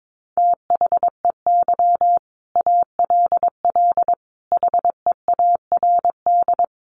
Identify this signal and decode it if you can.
Morse code, heard as a keyed tone, about 22 wpm, 700 Hz. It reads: THEY ALL HEARD
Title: Sentences from Top 400 Words 22wpm